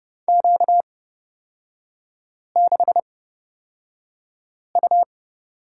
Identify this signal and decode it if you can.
Morse code, heard as a keyed tone, about 31 wpm, spaced as for 5 wpm, 690 Hz. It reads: Q6U
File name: course / 30:10wpm / 236 Lesson 38e - Letter q - Three Characters 30wpm